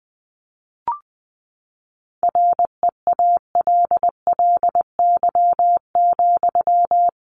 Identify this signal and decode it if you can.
Morse code, heard as a keyed tone, about 20 wpm, 700 Hz. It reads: REALLY,
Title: Sentences from Top 1400 Words 20wpm